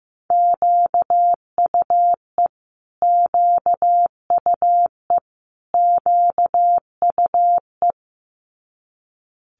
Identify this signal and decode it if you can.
Morse code, heard as a keyed tone, about 15 wpm, 700 Hz. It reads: QUE QUE QUE
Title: Multi-letter Phonemes Repeated 3x 15wpm